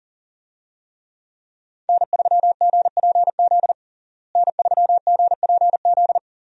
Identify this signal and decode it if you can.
Morse code, heard as a keyed tone, about 40 wpm, 700 Hz. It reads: N3GPZ N3GPZ
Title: USA Call Signs - 5 Characters - Repeated 2x - 40wpm